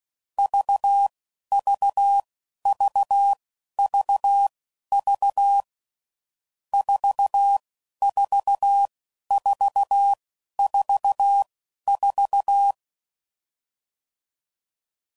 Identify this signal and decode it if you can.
Morse code, heard as a keyed tone, about 16 wpm, spaced as for 11 wpm, 800 Hz. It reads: VVVVV 44444
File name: Difficult Combos